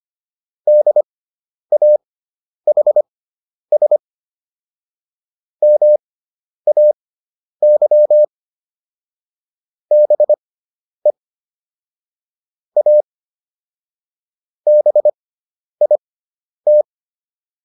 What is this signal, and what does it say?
Morse code, heard as a keyed tone, about 25 wpm, spaced as for 10 wpm, 600 Hz. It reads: DAHS MAY BE A BIT